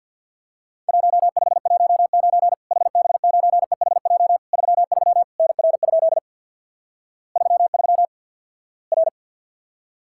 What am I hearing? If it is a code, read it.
Morse code, heard as a keyed tone, about 50 wpm, 700 Hz. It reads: 1519 HB9EHJ 33 NR? 33 R
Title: Worked All Europe DX Contest - QTC Exchange with Repeats - 50wpm